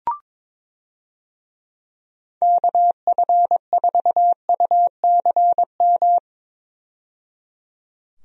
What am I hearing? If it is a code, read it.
Morse code, heard as a keyed tone, about 21 wpm, 700 Hz. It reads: KF4UCM
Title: USA Call Signs - 6 Characters - 22wpm